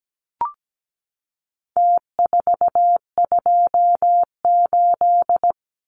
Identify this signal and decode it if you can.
Morse code, heard as a keyed tone, about 17 wpm, 700 Hz. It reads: T428
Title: Numbers - Contest Style - 4-Digit 17wpm